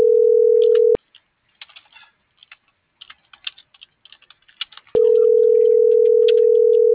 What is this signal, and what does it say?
Ringback tone (North America), 440 + 480 Hz.